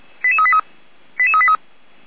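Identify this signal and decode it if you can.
Five-tone selective call (ZVEI), heard as quick runs of five tones, tones of about 80 ms. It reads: 88383 88383
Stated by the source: ZVEI five-tone selective call received on an SDR, archived by Feuerwehr Orlamünde.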